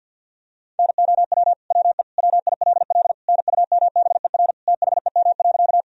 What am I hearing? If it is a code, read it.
Morse code, heard as a keyed tone, about 49 wpm, 700 Hz. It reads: NOW WE WILL NUMBER THEM.
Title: Sentences from Top 1000 Words - Review Part 1 - 50wpm